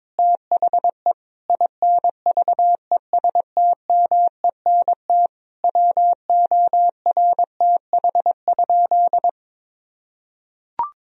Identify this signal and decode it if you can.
Morse code, heard as a keyed tone, about 21 wpm, 700 Hz. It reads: THE INVESTMENT WORTH?
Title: Sentences from Top 1000 Words - Review Part 4 - 22wpm